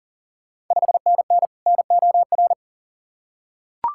Morse code, heard as a keyed tone, about 42 wpm, 690 Hz. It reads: 5NN NOR
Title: Ontario QSO Party Exchange 40wpm